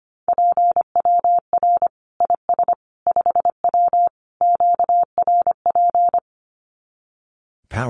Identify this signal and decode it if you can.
Morse code, heard as a keyed tone, about 25 wpm, 700 Hz. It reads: PWR IS 5W QRP
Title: Common QSO Phrases 25wpm